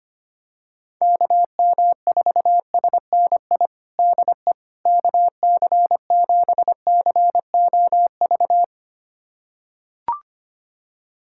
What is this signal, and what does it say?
Morse code, heard as a keyed tone, about 25 wpm, 700 Hz. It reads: KM4SNI DE KC7COV
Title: USA Callsign DE Callsign 25wpm